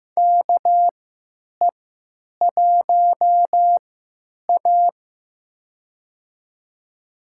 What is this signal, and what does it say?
Morse code, heard as a keyed tone, about 15 wpm, spaced as for 9 wpm, 700 Hz. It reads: KE1A
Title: USA Call Signs - 4 Characters - 3x Character Spacing - 15wpm